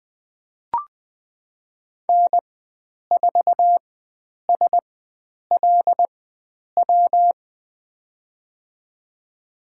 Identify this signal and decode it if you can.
Morse code, heard as a keyed tone, about 20 wpm, spaced as for 9 wpm, 700 Hz. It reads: N4SLW
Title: USA Call Signs - 5 Characters - 4x Character Spacing - 20wpm